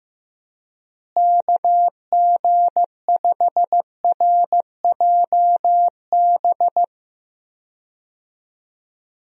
Morse code, heard as a keyed tone, about 15 wpm, 700 Hz. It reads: KG5RJB
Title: USA Call Signs - 6 Characters - Repeated 3x - 15wpm